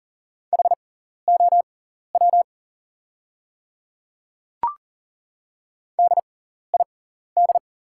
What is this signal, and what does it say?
Morse code, heard as a keyed tone, about 37 wpm, spaced as for 14 wpm, 700 Hz. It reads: HOW DID
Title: 6x Character Spacing - Top 100 Words - 40wpm